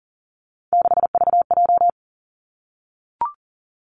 Morse code, heard as a keyed tone, about 37 wpm, 700 Hz. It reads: 6VJ
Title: Three Challenging Letter-Number Combinations 40wpm